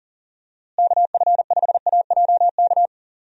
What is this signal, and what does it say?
Morse code, heard as a keyed tone, about 40 wpm, 700 Hz. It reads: KF5AJK
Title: USA Call Signs - 6 Characters - Repeated 2x - 40wpm